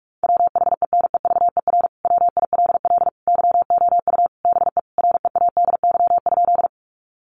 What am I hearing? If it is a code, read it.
Morse code, heard as a keyed tone, about 44 wpm, 700 Hz. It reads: WHENEVER WILL YOU BE READY?
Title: Sentences from Top 1200 Words 45wpm